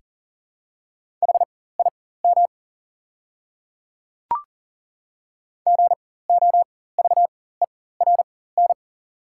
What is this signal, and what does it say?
Morse code, heard as a keyed tone, about 40 wpm, spaced as for 19 wpm, 690 Hz. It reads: HIM GOVERN
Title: 4x Character Spacing - Top 500 Words - 40wpm